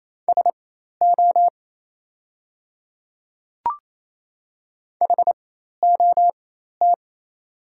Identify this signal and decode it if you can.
Morse code, heard as a keyed tone, about 27 wpm, spaced as for 13 wpm, 690 Hz. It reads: SO HOT
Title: 4x Character Spacing - Top 100 Words - 28wpm